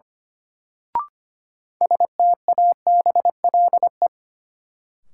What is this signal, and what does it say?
Morse code, heard as a keyed tone, about 25 wpm, 700 Hz. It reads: STABLE
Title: Sentences from Top 1500 Words - New Word Drill Alt - 25wpm